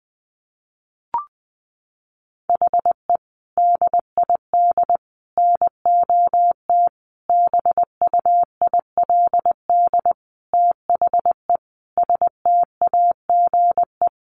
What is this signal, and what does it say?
Morse code, heard as a keyed tone, about 20 wpm, 700 Hz. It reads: HE DID NOT BUILD THE STAGE
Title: Sentences from Top 1200 Words 20wpm